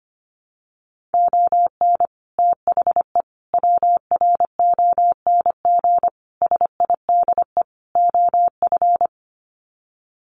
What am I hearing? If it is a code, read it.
Morse code, heard as a keyed tone, about 25 wpm, 700 Hz. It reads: ON THE WRONG SIDE OF
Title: Sets of 5 Words Encore 25wpm